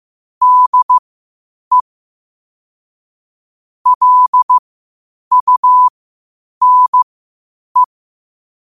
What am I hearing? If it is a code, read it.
Morse code, heard as a keyed tone, about 15 wpm, spaced as for 8 wpm, 1000 Hz. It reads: DE LUNE
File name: qotdfiles / 8wpm